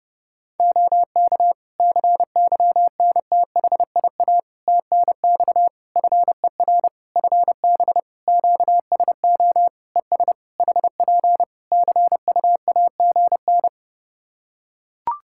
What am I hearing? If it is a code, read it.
Morse code, heard as a keyed tone, about 30 wpm, 700 Hz. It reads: OK CYNTHIA TNX FER FB QSO ES HP CUAGN
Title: K1USN SST QSO Practice 30wpm